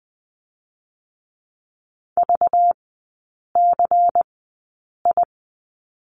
Morse code, heard as a keyed tone, about 20 wpm, spaced as for 8 wpm, 690 Hz. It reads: VCI
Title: 205 Lesson 33e - Letter j - Three Characters 20wpm